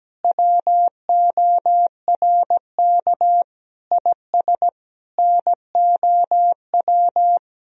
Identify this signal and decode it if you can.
Morse code, heard as a keyed tone, about 17 wpm, 700 Hz. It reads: WORK IS NOW
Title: Sentences from Top 1000 Words - Review Part 2 - 17wpm